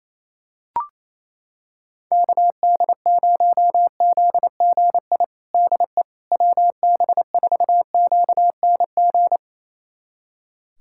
Morse code, heard as a keyed tone, about 28 wpm, 700 Hz. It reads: KD0ZGI DE WB4QNG
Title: USA Callsign DE Callsign - 6 Characters - 28wpm